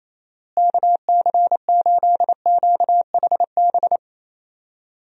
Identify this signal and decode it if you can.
Morse code, heard as a keyed tone, about 27 wpm, 700 Hz. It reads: KC8QHB
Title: USA Call Signs - 6 Characters - Repeated 3x - 28wpm